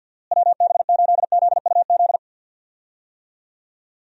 Morse code, heard as a keyed tone, about 51 wpm, 700 Hz. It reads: WB8ZUZ
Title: USA Call Signs - 6 Characters - Repeated 2x - 50wpm